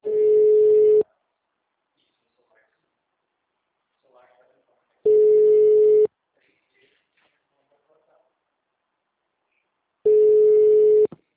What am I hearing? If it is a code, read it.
Ringback tone (Europe, 425 Hz), with speech.